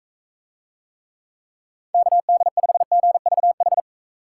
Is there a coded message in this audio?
Yes: Morse code from 1.9 to 3.8 s, heard as a keyed tone, about 44 wpm, 700 Hz. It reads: KD5GVH